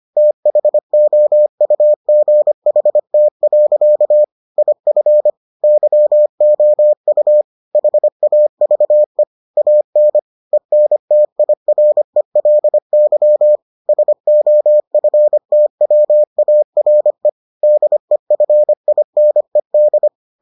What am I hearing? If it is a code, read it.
Morse code, heard as a keyed tone, about 25 wpm, 600 Hz. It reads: THOUGHT. IF YOU HAVE AN ENTIRELY SOFTWARE DEFINED